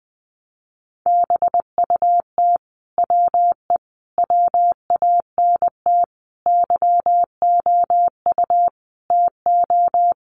Morse code, heard as a keyed tone, about 20 wpm, 700 Hz. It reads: BUT WE WANT YOU TO
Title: Sentences from Top 700 Words 20wpm